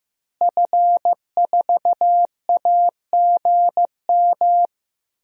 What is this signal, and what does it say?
Morse code, heard as a keyed tone, about 15 wpm, 700 Hz. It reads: F4AGM
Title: DXCC-100 Call Signs - 5 Characters - Repeated 3x - 15wpm